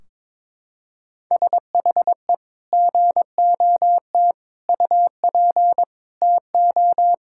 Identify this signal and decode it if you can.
Morse code, heard as a keyed tone, about 21 wpm, 700 Hz. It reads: SHE GOT UP TO
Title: Sentences from Top 1600 Words 22wpm